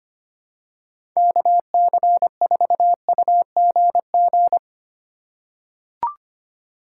Morse code, heard as a keyed tone, about 25 wpm, 700 Hz. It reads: KC4UGG
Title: USA Call Signs - 6 Characters - 2x Character Spacing - 25wpm